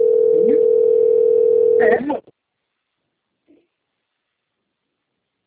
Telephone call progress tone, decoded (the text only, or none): ringback tone